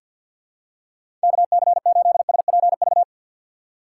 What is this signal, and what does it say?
Morse code, heard as a keyed tone, about 51 wpm, 700 Hz. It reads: KC8SPV
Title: USA Call Signs - 6 Characters - 6x Character Spacing - 50wpm